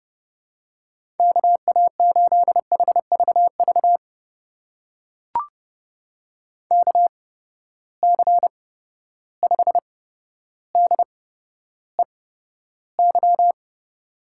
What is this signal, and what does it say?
Morse code, heard as a keyed tone, about 30 wpm, spaced as for 13 wpm, 700 Hz. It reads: KA8HVV KC5DEY
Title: USA Call Signs - 8x Character Spacing - 30wpm